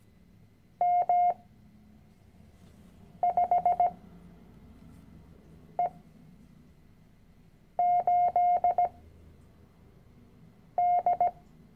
Morse code, heard as a keyed tone, about 17 wpm, spaced as for 4 wpm, 710 Hz. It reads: M5E8D
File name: CW-Online-Kurs / Anfänger / Lektion 010